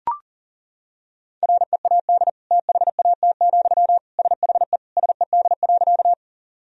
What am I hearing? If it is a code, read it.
Morse code, heard as a keyed tone, about 40 wpm, 700 Hz. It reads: READ THAT, SHE SED.
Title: Sentences from Top 200 Words and Abbreviations 40wpm